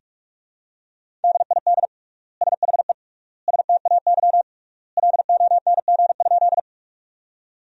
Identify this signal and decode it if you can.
Morse code, heard as a keyed tone, about 41 wpm, 700 Hz. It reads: DID SHE STAY LONG?
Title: 3x Word-Spacing - Sentences from Top 500 Words - 45wpm